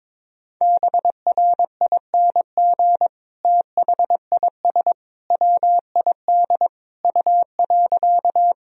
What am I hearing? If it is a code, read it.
Morse code, heard as a keyed tone, about 22 wpm, 700 Hz. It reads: BRING THIS WID U.
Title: Sentences from Top 500 Words and Abbreviations 22wpm